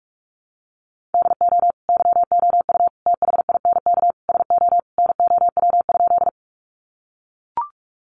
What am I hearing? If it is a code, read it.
Morse code, heard as a keyed tone, about 44 wpm, 700 Hz. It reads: DO YOU THINK SO NOW?
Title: Sentences from Top 200 Words 45wpm